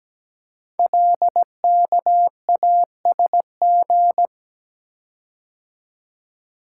Morse code, heard as a keyed tone, about 17 wpm, 700 Hz. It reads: LKASG
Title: Code Groups - 5 Letters - 17wpm